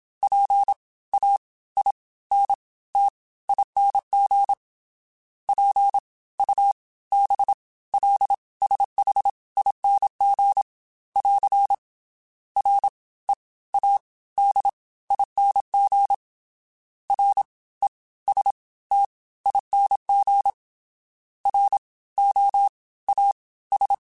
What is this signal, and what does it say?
Morse code, heard as a keyed tone, about 27 wpm, spaced as for 18 wpm, 800 Hz. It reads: PAINTING PUBLSHING <AR> READING RESTING ROAS